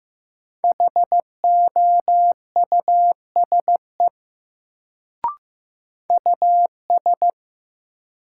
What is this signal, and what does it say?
Morse code, heard as a keyed tone, about 15 wpm, 700 Hz. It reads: HOUSE US
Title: Sets of 2 Words - New Word Drill - 15wpm